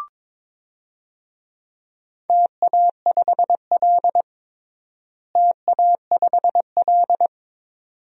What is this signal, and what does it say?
Morse code, heard as a keyed tone, about 22 wpm, 700 Hz. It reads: TA5L TA5L
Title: DXCC-100 Call Signs - 4 Characters - Repeated 2x - 22wpm